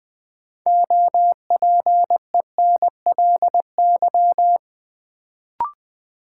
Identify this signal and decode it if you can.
Morse code, heard as a keyed tone, about 20 wpm, 700 Hz. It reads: OPENLY
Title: Build A Word - 2 Letters At A Time - 6-Letter Words - 20wpm